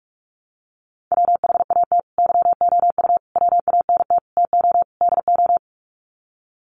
Morse code, heard as a keyed tone, about 44 wpm, 700 Hz. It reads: WHAT YOU WANT TO DO
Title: Sets of 5 Words Encore 45wpm